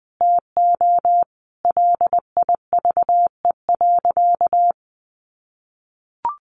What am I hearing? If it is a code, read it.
Morse code, heard as a keyed tone, about 20 wpm, 700 Hz. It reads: TO LIVE.
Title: Sentences from Top 300 Words 20wpm